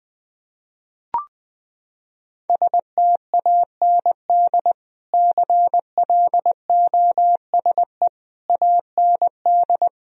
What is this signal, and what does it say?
Morse code, heard as a keyed tone, about 20 wpm, 700 Hz. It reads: STAND CLOSE AND
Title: Sentences from Top 1000 Words - Review Part 2 - 20wpm